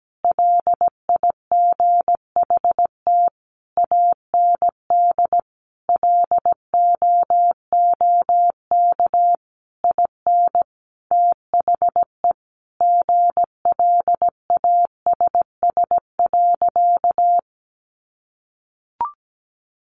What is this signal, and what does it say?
Morse code, heard as a keyed tone, about 17 wpm, 700 Hz. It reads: LIGHT AND LOOK IN THE GLASS.
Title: Sentences from Top 1300 Words 17wpm